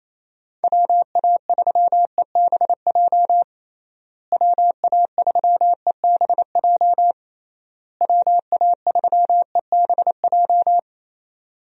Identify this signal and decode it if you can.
Morse code, heard as a keyed tone, about 28 wpm, 700 Hz. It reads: WA3EBJ WA3EBJ WA3EBJ
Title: USA Call Signs - 6 Characters - Repeated 3x - 28wpm